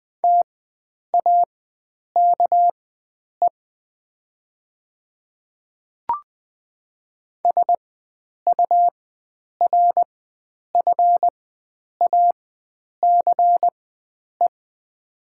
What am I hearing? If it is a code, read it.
Morse code, heard as a keyed tone, about 20 wpm, spaced as for 9 wpm, 700 Hz. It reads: TAKE SURFACE
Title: 4x Character Spacing - Top 500 Words - 20wpm